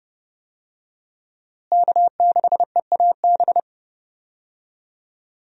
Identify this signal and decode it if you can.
Morse code, heard as a keyed tone, about 30 wpm, 700 Hz. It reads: K6EAB
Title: USA Call Signs - 5 Characters - Repeated 2x - 30wpm